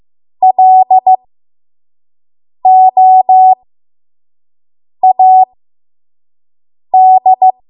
Morse code, heard as a keyed tone, about 15 wpm, spaced as for 5 wpm, 750 Hz. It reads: LOAD
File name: Morse / 05WPM